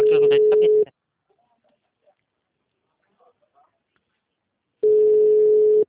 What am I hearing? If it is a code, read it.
Ringback tone (Europe, 425 Hz), with speech.
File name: out- -2016- - - .9026